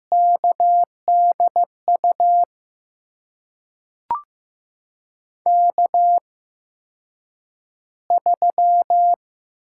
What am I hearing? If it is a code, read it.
Morse code, heard as a keyed tone, about 15 wpm, spaced as for 8 wpm, 700 Hz. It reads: KDU K3
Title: USA Call Signs - 5 Characters - 8x Character Spacing - 15wpm